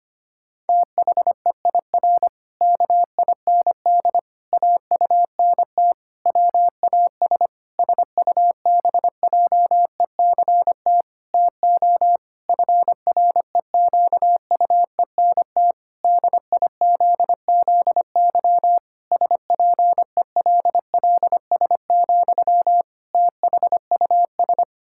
Morse code, heard as a keyed tone, about 25 wpm, 700 Hz. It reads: THEIR KIND AUNT WAS SUBJECT TO FREQUENT DIZZY SPELLS, THUS
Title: Pangram Challenge v2 - 25wpm